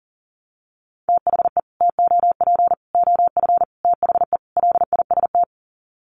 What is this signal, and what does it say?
Morse code, heard as a keyed tone, about 40 wpm, 700 Hz. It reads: THE TOP OF THE LIST
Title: Sets of 5 Words Encore2 40wpm